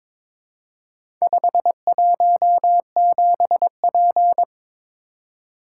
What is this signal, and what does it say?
Morse code, heard as a keyed tone, about 22 wpm, 690 Hz. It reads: 517P
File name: practice / 22wpm / Four Letter-Number Combinations - 22wpm